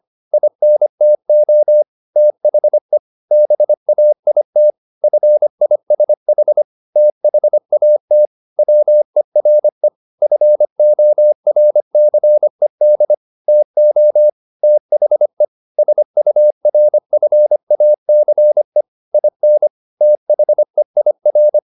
Morse code, heard as a keyed tone, about 25 wpm, 600 Hz. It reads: INTO THE BAIT FISH THAT WERE FORCED TO THE SURFACE IN THEIR
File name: the old man and the sea part 1 25wpm